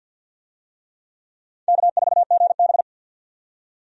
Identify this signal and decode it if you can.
Morse code, heard as a keyed tone, about 54 wpm, 700 Hz. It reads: K4GB
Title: USA Call Signs - 4 Characters - 50wpm